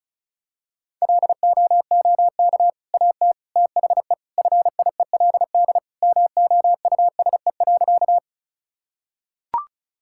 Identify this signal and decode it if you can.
Morse code, heard as a keyed tone, about 36 wpm, 700 Hz. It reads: LOOK AT THE FIELD MOUSE.
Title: Sentences from Top 1000 Words - Review Part 5 - 35wpm